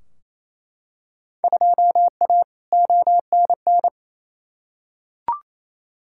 Morse code, heard as a keyed tone, about 28 wpm, 700 Hz. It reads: 2A ONN
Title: Field Day Exchanges Encore - 28wpm